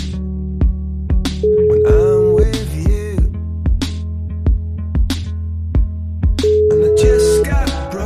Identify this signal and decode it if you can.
Ringback tone (Europe, 425 Hz).